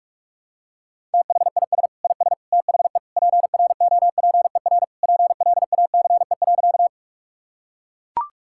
Morse code, heard as a keyed tone, about 41 wpm, 700 Hz. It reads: THIS IS THE PROPER PLACE.
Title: Sentences from Top 1000 Words - Review Part 5 - 45wpm